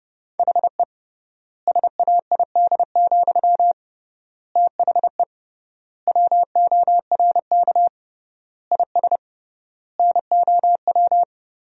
Morse code, heard as a keyed tone, about 32 wpm, 700 Hz. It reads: HE SAID, THE WORK IS NOW
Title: 3x Word-Spacing - Sentences from Top 400 Words - 30wpm